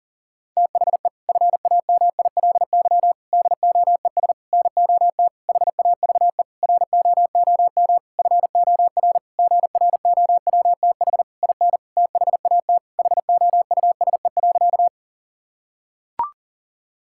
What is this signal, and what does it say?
Morse code, heard as a keyed tone, about 40 wpm, 700 Hz. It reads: THE FAMILY DOES NOT HAVE ROOM FOR GROWTH IN THAT HOUSE.
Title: Sentences from Top 1000 Words - Review Part 5 - 40wpm